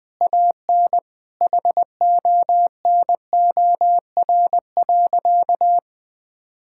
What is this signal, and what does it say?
Morse code, heard as a keyed tone, about 20 wpm, 700 Hz. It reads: AN HONOR.
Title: Sentences with 5-Word Phrases - Spoken Beginning - 20wpm